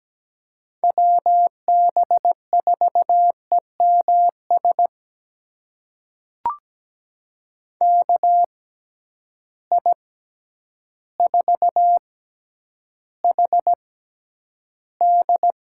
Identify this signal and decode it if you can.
Morse code, heard as a keyed tone, about 17 wpm, spaced as for 9 wpm, 700 Hz. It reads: WB4EMS KI4HD
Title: USA Call Signs - 6 Characters - 6x Character Spacing - 17wpm